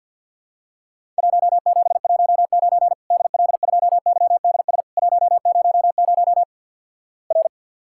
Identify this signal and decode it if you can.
Morse code, heard as a keyed tone, about 52 wpm, 700 Hz. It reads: 1719 DL2YDS 100 R
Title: Worked All Europe DX Contest - QTC Exchange with Repeats - 50wpm